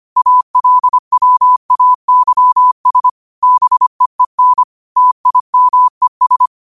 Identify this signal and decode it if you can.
Morse code, heard as a keyed tone, about 25 wpm, 1000 Hz. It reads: ALWAYS BEEN TIMES
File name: qotdfiles / 25wpm